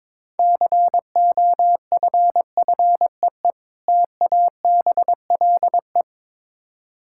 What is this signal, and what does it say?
Morse code, heard as a keyed tone, about 22 wpm, 700 Hz. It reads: COFFEE TABLE
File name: Easy Words - Speed-Racing with Hints - 22wpm